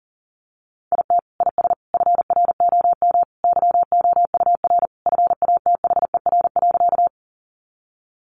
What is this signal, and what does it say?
Morse code, heard as a keyed tone, about 40 wpm, 700 Hz. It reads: IT IS FROM YOUR FATHER.
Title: Sentences from Top 300 Words 40wpm